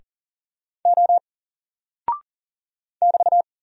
Morse code, heard as a keyed tone, about 39 wpm, 700 Hz. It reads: O <BT>
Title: Single Character (Letter:Number:Punctuation:Prosign) Alt 40wpm